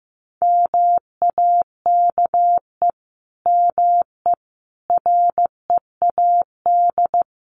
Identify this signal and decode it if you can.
Morse code, heard as a keyed tone, about 15 wpm, 700 Hz. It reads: MAKE ME READ
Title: Sentences from Top 1100 Words 15wpm